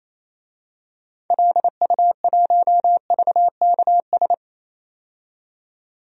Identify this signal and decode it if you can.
Morse code, heard as a keyed tone, about 28 wpm, 700 Hz. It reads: LU1VKS
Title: DXCC-100 Call Signs - 6 Characters - 28wpm